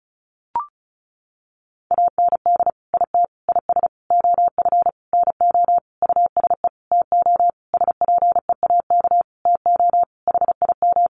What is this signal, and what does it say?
Morse code, heard as a keyed tone, about 34 wpm, 700 Hz. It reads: AND IT IS OF NO USE TO SPEAK TO HIM
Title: Sentences from Top 1100 Words 35wpm